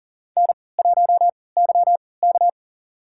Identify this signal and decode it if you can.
Morse code, heard as a keyed tone, about 40 wpm, spaced as for 23 wpm, 700 Hz. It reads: N1YK
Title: USA Call Signs - 4 Characters - 3x Character Spacing - 40wpm